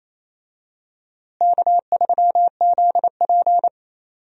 Morse code, heard as a keyed tone, about 28 wpm, 700 Hz. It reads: K3ZP
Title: USA Call Signs - 4 Characters - Repeated 2x - 28wpm